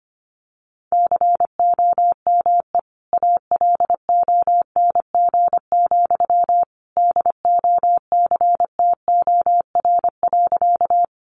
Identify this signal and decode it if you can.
Morse code, heard as a keyed tone, about 25 wpm, 700 Hz. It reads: COME ALONG, DOCTOR.
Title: Sentences from Top 1500 Words 25wpm